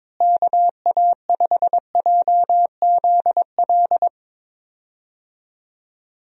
Morse code, heard as a keyed tone, about 22 wpm, 700 Hz. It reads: KA5JZL
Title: USA Call Signs - 6 Characters - Repeated 2x - 22wpm